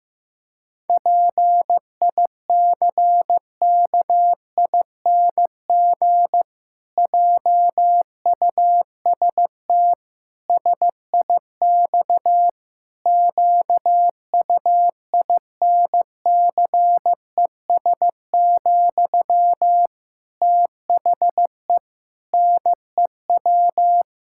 Morse code, heard as a keyed tone, about 15 wpm, 700 Hz. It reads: PICKING JUST SIX QUINCES, THE NEW
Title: Pangram Challenge v2 - 15wpm